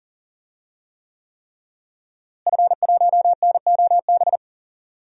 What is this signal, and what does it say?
Morse code, heard as a keyed tone, about 39 wpm, 700 Hz. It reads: F1NOB
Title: DXCC-100 Call Signs - 5 Characters - Repeated 3x - 40wpm